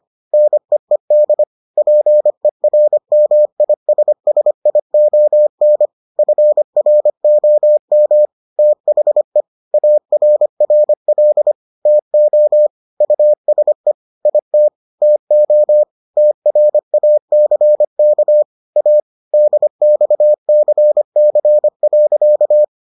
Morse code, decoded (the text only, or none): NEED PERMISSION FROM THE ARRL TO USE IT TO TRACK A DXCC.